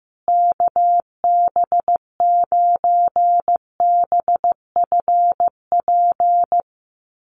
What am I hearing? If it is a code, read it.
Morse code, heard as a keyed tone, about 15 wpm, 700 Hz. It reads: KB9BFP
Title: USA Call Signs Repeated 3x 15wpm